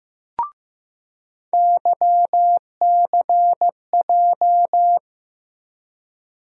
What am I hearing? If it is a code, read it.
Morse code, heard as a keyed tone, about 15 wpm, 700 Hz. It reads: YCJ
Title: USA Call Sign Postfixes - 3 Letters 15wpm